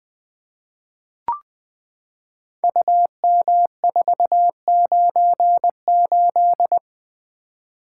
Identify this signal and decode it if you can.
Morse code, heard as a keyed tone, about 20 wpm, 700 Hz. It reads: UM498
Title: Code Groups - 5 Letter:Numbers - 20wpm